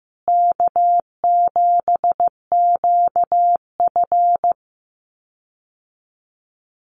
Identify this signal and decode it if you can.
Morse code, heard as a keyed tone, about 15 wpm, 700 Hz. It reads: K7QF
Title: USA Call Signs Repeated 2x 15wpm